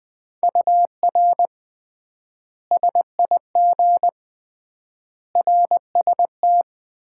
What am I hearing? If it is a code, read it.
Morse code, heard as a keyed tone, about 21 wpm, 700 Hz. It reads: UR SIG RST
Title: Common QSO Phrases - 3x Word-Spacing - 20wpm